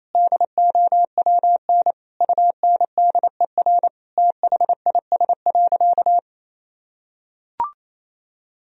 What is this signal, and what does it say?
Morse code, heard as a keyed tone, about 28 wpm, 700 Hz. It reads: DOWN UNDER THIS.
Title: Sentences from Top 1000 Words - Review Part 1 - 28wpm